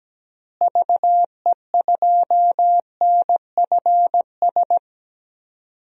Morse code, heard as a keyed tone, about 17 wpm, 700 Hz. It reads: VE2NFS
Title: DXCC-100 Call Signs - 6 Characters - Repeated 2x - 17wpm